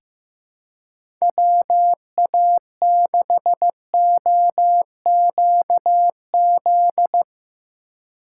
Morse code, heard as a keyed tone, about 15 wpm, 700 Hz. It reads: WA6OQZ
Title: USA Call Signs - 6 Characters - 4x Character Spacing - 15wpm